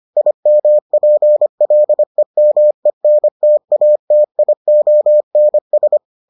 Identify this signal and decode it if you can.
Morse code, heard as a keyed tone, about 25 wpm, 600 Hz. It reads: IMPLEMENTATIONS